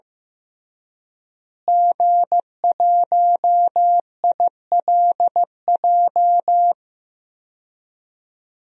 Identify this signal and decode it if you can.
Morse code, heard as a keyed tone, about 15 wpm, 700 Hz. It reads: G1ILJ
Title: DXCC-100 Call Signs - 5 Characters - Repeated 2x - 15wpm